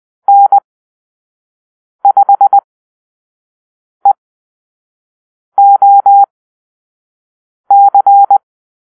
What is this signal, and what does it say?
Morse code, heard as a keyed tone, about 20 wpm, spaced as for 5 wpm, 800 Hz. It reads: N5EOC
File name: arrl / Call-Sign Drill-1